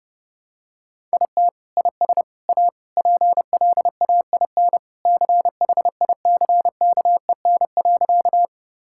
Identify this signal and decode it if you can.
Morse code, heard as a keyed tone, about 32 wpm, 700 Hz. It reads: IT IS A PLAIN CHICKEN.
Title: Sentences from Top 1000 Words - Review Part 5 - 30wpm